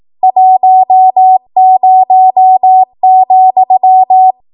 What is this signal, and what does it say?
Morse code, heard as a keyed tone, about 18 wpm, 750 Hz. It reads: 10,
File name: Archive / 18 WPM / 18WPM